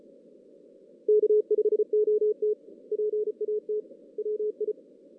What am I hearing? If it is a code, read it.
Morse code, heard as a keyed tone, about 33 wpm, 470 Hz. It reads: K5OT PAT WI
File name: 2016 JAN CW NAQP / 2016 Jan naqp k5ot - 0-6